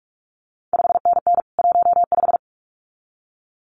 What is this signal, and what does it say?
Morse code, heard as a keyed tone, about 46 wpm, 700 Hz. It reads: 5NN 15
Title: CQ WW DX Contest Exchange - 45wpm